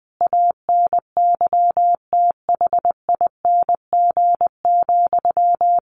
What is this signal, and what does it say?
Morse code, heard as a keyed tone, about 19 wpm, 700 Hz. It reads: ANYTHING,
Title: Sentences from Top 1400 Words 20wpm